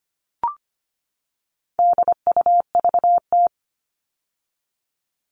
Morse code, heard as a keyed tone, about 25 wpm, 700 Hz. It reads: DUVT
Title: All Cut Numbers - 4-Digit 25wpm